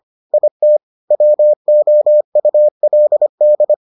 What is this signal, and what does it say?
Morse code, heard as a keyed tone, about 25 wpm, 600 Hz. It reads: IT WOULD